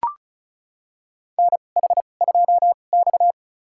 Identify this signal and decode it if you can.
Morse code, heard as a keyed tone, about 36 wpm, spaced as for 26 wpm, 700 Hz. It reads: NH2X
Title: USA Call Signs - 4 Characters - 2x Character Spacing - 35wpm